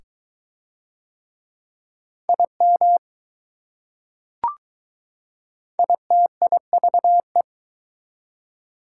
Morse code, heard as a keyed tone, about 21 wpm, 700 Hz. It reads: IM ITIVE